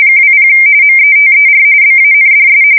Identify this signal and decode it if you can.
RTTY, about 45.45 baud. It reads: 20 METER BANDS
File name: fsk170 / teletype-RTTY